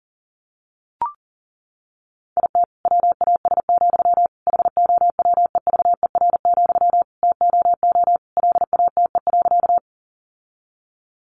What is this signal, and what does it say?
Morse code, heard as a keyed tone, about 40 wpm, 700 Hz. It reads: IT WAS, HOWEVER, TOO LATE.
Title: Sentences from Top 600 Words 40wpm